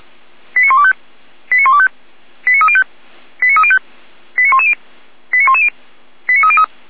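Five-tone selective call (ZVEI), heard as quick runs of five tones, tones of about 70 ms. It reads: 89125 89125 89305 89305 89110 89110 89303
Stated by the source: ZVEI five-tone selective call received on an SDR.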